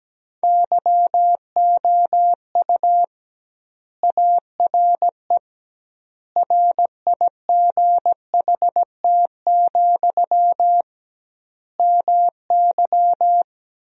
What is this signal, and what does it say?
Morse code, heard as a keyed tone, about 17 wpm, 700 Hz. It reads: YOU ARE RIGHT, MY
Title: 2x Word-Spacing - Sentences from Top 400 Words - 17wpm